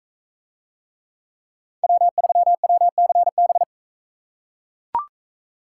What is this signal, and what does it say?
Morse code, heard as a keyed tone, about 42 wpm, 700 Hz. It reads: W3WCB